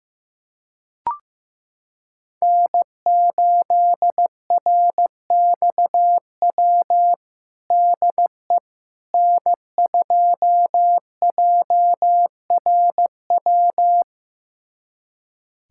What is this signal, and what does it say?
Morse code, heard as a keyed tone, about 16 wpm, 700 Hz. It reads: N8RXW DE N2JRW
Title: USA Callsign DE Callsign - 5 Characters - 15wpm